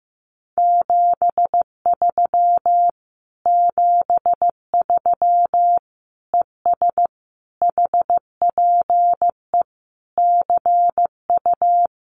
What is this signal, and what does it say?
Morse code, heard as a keyed tone, about 14 wpm, 700 Hz. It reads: 73 73 ES HPE CU
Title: Common QSO Phrases Encore 15wpm